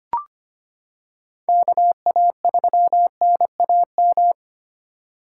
Morse code, heard as a keyed tone, about 25 wpm, 700 Hz. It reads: KA3NAM
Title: K1USN SST QSO Practice - 25wpm